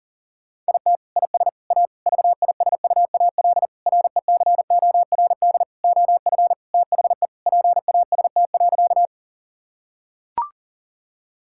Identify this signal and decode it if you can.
Morse code, heard as a keyed tone, about 40 wpm, 700 Hz. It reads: IT IS A VISUAL RECORD OF THE PAST.
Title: Sentences from Top 1000 Words - Review Part 6 - 40wpm